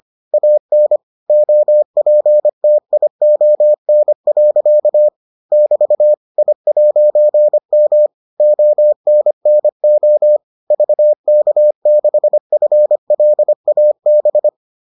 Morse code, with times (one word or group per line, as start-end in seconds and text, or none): 0.3-1.0 AN
1.3-5.1 OPTION.
5.5-6.1 <BT>
6.4-8.1 I'M
8.4-10.4 ONNO
10.7-14.5 VK6FLAB